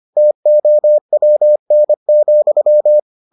Morse code, heard as a keyed tone, about 25 wpm, 600 Hz. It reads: TOWN,